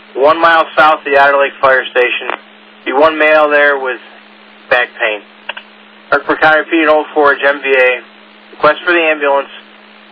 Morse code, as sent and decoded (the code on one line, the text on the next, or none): none